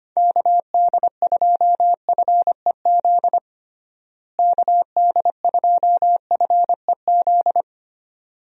Morse code, heard as a keyed tone, about 25 wpm, 700 Hz. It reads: KD2FEZ KD2FEZ
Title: USA Call Signs - 6 Characters - Repeated 2x - 25wpm